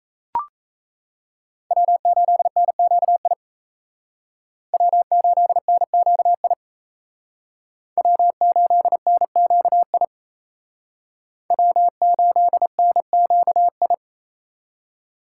Morse code, heard as a keyed tone, about 35 wpm, 700 Hz. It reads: W8NQI W8NQI W8NQI W8NQI